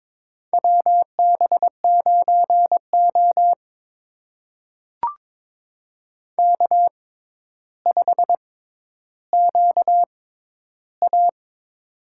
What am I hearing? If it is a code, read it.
Morse code, heard as a keyed tone, about 22 wpm, spaced as for 11 wpm, 700 Hz. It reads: WB9O K5QA
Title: USA Call Signs - 4 Characters - 6x Character Spacing - 22wpm